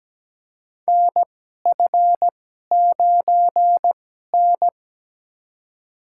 Morse code, heard as a keyed tone, about 17 wpm, spaced as for 12 wpm, 700 Hz. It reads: NF9N
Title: USA Call Signs - 4 Characters - 2x Character Spacing - 17wpm